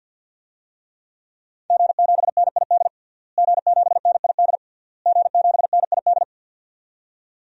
Morse code, heard as a keyed tone, about 51 wpm, 700 Hz. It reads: G7NID G7NID G7NID
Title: DXCC-100 Call Signs - 5 Characters - Repeated 3x - 50wpm